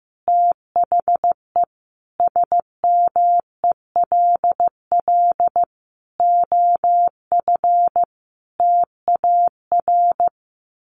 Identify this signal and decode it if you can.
Morse code, heard as a keyed tone, about 15 wpm, 700 Hz. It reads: THE SMELL OF TAR
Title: A Sound Of Thunder 15wpm